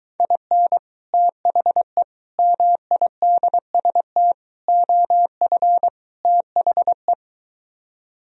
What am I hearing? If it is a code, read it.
Morse code, heard as a keyed tone, about 23 wpm, 700 Hz. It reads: IN THE MIDST OF THE